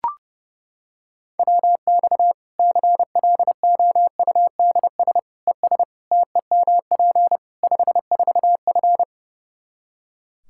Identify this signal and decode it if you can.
Morse code, heard as a keyed tone, about 30 wpm, 700 Hz. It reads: WX CLOUDS ES TEMP 54F
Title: Practice QSO - Exchange 2 - Info - 30wpm